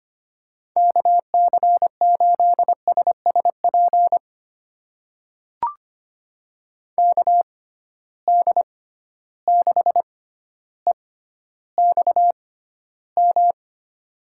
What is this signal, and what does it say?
Morse code, heard as a keyed tone, about 25 wpm, spaced as for 13 wpm, 700 Hz. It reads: KC8SSP KD6EXM
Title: USA Call Signs - 6 Characters - 6x Character Spacing - 25wpm